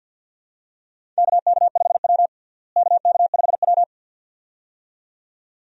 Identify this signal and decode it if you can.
Morse code, heard as a keyed tone, about 50 wpm, 690 Hz. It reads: KK5W KK5W
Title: USA Call Signs - 4 Characters - Repeated 2x - 50wpm